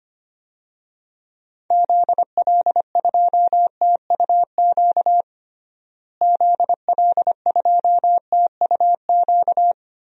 Morse code, heard as a keyed tone, about 25 wpm, 700 Hz. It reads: ZL2TUQ ZL2TUQ
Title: DXCC-100 Call Signs - 6 Characters - Repeated 2x - 25wpm